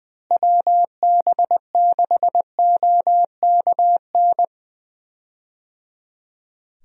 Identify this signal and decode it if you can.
Morse code, heard as a keyed tone, about 20 wpm, 700 Hz. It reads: WB6OKN
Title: USA Call Signs - 6 Characters - Repeated 3x - 20wpm